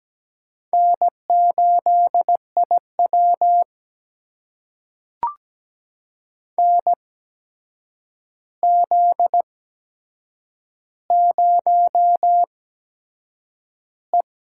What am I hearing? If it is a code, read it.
Morse code, heard as a keyed tone, about 17 wpm, spaced as for 7 wpm, 700 Hz. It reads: N8IW NZ0E
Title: USA Call Signs - 4 Characters - 8x Character Spacing - 17wpm